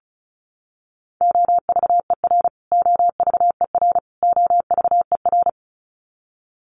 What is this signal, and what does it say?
Morse code, heard as a keyed tone, about 34 wpm, 700 Hz. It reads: OVER OVER OVER
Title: Top 200 Words Repeated 35wpm